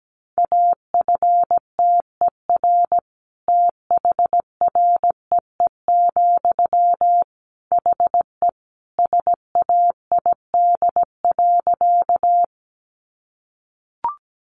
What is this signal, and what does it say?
Morse code, heard as a keyed tone, about 17 wpm, 700 Hz. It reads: AFTER THREE, HE SAID.
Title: Sentences from Top 200 Words 17wpm